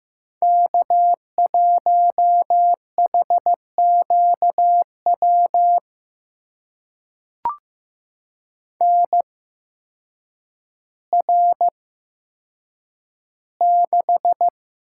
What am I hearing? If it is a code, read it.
Morse code, heard as a keyed tone, about 15 wpm, spaced as for 8 wpm, 700 Hz. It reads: K1HQW NR6
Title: USA Call Signs - 8x Character Spacing - 15wpm